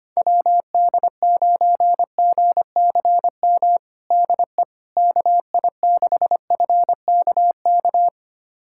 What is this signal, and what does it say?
Morse code, heard as a keyed tone, about 25 wpm, 700 Hz. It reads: WD9GCM DE KI6FKK
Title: USA Callsign DE Callsign - 6 Characters - 25wpm